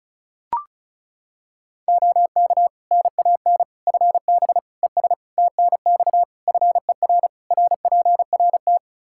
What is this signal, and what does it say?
Morse code, heard as a keyed tone, about 36 wpm, 700 Hz. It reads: OK NAN FB ES TNX FER RPRT
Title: Practice QSO - Exchange 2 - Info - 35wpm